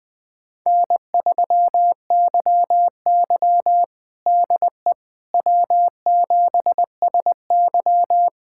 Morse code, heard as a keyed tone, about 21 wpm, 700 Hz. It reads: N3YY DE W7SY
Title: USA Callsign DE Callsign - 4 Characters - 20wpm